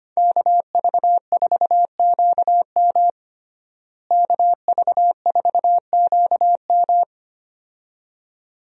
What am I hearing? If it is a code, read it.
Morse code, heard as a keyed tone, about 25 wpm, 700 Hz. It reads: KV4QM KV4QM
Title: USA Call Signs - 5 Characters - Repeated 2x - 25wpm